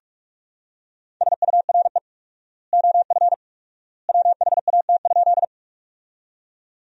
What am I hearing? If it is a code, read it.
Morse code, heard as a keyed tone, about 41 wpm, 700 Hz. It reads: SURE OF WHAT?
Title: 4x Word-Spacing - Sentences from Top 300 Words - 45wpm